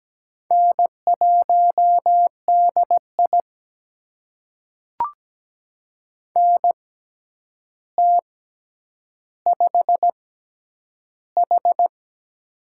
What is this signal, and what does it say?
Morse code, heard as a keyed tone, about 17 wpm, spaced as for 9 wpm, 700 Hz. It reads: N1DI NT5H
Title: USA Call Signs - 4 Characters - 6x Character Spacing - 17wpm